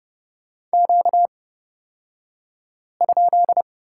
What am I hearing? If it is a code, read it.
Morse code, heard as a keyed tone, about 30 wpm, spaced as for 5 wpm, 700 Hz. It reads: Q?
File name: Lesson 38d - Letter q - Two Characters 30wpm